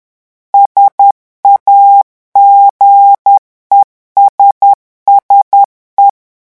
Morse code, heard as a keyed tone, about 10 wpm, 800 Hz. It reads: SAGESSE
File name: morse mp3 / revision francais 10 mots minute